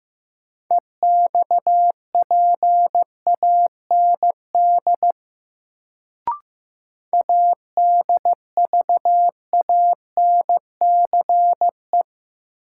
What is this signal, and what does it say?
Morse code, heard as a keyed tone, about 15 wpm, 700 Hz. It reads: EXPAND ADVANCE
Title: Sentences from Top 1400 Words - New Word Drill - 15wpm